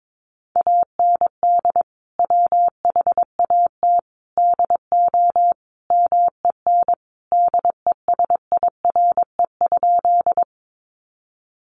Morse code, heard as a keyed tone, about 22 wpm, 700 Hz. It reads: AND WHAT DO MEN DESIRE?
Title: Sentences from Top 1100 Words 22wpm